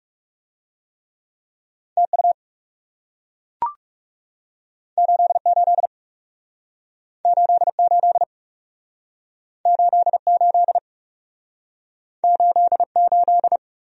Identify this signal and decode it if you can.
Morse code, heard as a keyed tone, about 37 wpm, 700 Hz. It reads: TU 88 88 88 88
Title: Speed-Racing Abbreviations - 30wpm